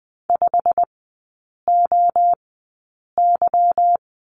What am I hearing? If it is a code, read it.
Morse code, heard as a keyed tone, about 20 wpm, spaced as for 8 wpm, 700 Hz. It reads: 5OY
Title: 205 Lesson 33e - Letter j - Three Characters 20wpm